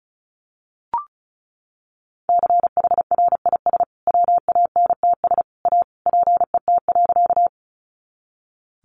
Morse code, heard as a keyed tone, about 34 wpm, 700 Hz. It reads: CHRIS WANTS A PET.
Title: Additional Easy Phrases 35wpm